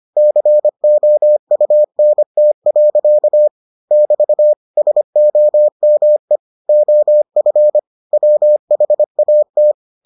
Morse code, heard as a keyed tone, about 25 wpm, 600 Hz. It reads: COUNT. <BT> SOME OF WHAT